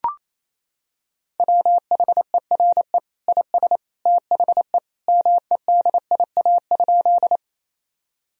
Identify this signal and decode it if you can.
Morse code, heard as a keyed tone, about 28 wpm, 700 Hz. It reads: WHERE IS THE MEDIA?
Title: Sentences from Top 1000 Words - Review Part 4 - 28wpm